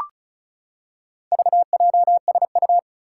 Morse code, heard as a keyed tone, about 36 wpm, 700 Hz. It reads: VJSU
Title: Four Challenging Letter-Number Combinations 35wpm